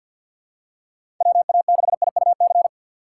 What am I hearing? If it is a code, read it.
Morse code, heard as a keyed tone, about 52 wpm, 700 Hz. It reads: WA6IUC
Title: USA Call Signs - 6 Characters - 8x Character Spacing - 50wpm